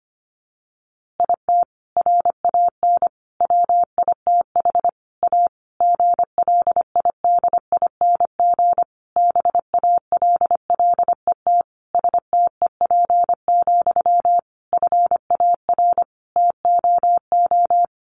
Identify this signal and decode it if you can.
Morse code, heard as a keyed tone, about 25 wpm, 700 Hz. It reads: IT RAN WITH A GLIDING BALLET STEP, FAR TOO
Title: A Sound Of Thunder 25wpm